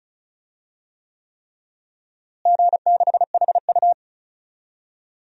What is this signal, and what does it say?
Morse code, heard as a keyed tone, about 34 wpm, 700 Hz. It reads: G6HU
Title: DXCC-100 Call Signs - 4 Characters - Repeated 3x - 35wpm